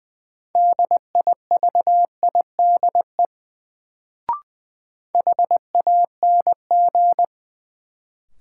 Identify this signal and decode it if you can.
Morse code, heard as a keyed tone, about 20 wpm, 700 Hz. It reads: DIVIDE HANG
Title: Sentences from Top 1600 Words - New Word Drill - 20wpm